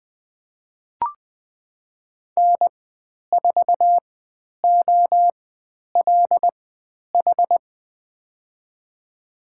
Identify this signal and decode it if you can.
Morse code, heard as a keyed tone, about 20 wpm, spaced as for 10 wpm, 700 Hz. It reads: N4OLH
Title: 109 Lesson 18g - Letter m - Call Signs 20wpm